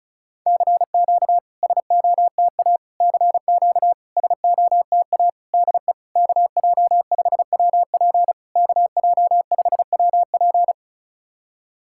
Morse code, heard as a keyed tone, about 36 wpm, 690 Hz. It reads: CQ SOTA CQ SOTA DE KJ5WP KJ5WP
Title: SOTA QSO Practice 35wpm